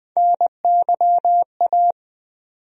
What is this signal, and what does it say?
Morse code, heard as a keyed tone, about 20 wpm, 700 Hz. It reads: NYA
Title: USA Call Sign Postfixes - 3 Letters 20wpm